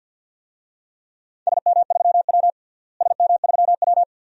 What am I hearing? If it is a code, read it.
Morse code, heard as a keyed tone, about 49 wpm, 700 Hz. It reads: SM3W SM3W
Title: DXCC-100 Call Signs - 4 Characters - Repeated 2x - 50wpm